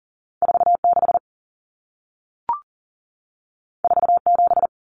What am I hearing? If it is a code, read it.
Morse code, heard as a keyed tone, about 40 wpm, 700 Hz. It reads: 46 47
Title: Numbers - 2-Digit 40wpm